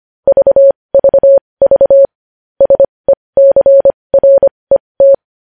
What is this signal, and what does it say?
Morse code, heard as a keyed tone, about 25 wpm, spaced as for 20 wpm, 550 Hz. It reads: VVV SECRET
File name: podcast / News headlines-1x-25x20